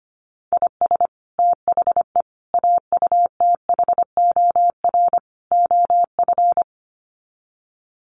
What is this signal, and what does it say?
Morse code, heard as a keyed tone, about 25 wpm, 700 Hz. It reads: IS THE AUTHOR OF
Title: Sets of 4 Words 25wpm